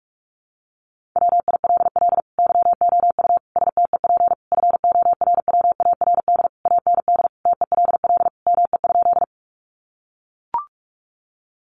Morse code, heard as a keyed tone, about 44 wpm, 700 Hz. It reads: WILL YOU STEP FORWARD AND TELL ME?
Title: Sentences from Top 1300 Words 45wpm